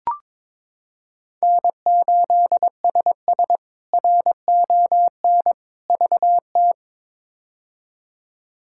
Morse code, heard as a keyed tone, about 21 wpm, 700 Hz. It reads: N8SS RON VT
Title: K1USN SST QSO Practice - 22wpm